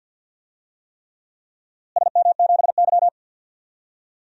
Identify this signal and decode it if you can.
Morse code, heard as a keyed tone, about 48 wpm, 700 Hz. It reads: SM7Y
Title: DXCC-100 Call Signs - 4 Characters - Repeated 3x - 50wpm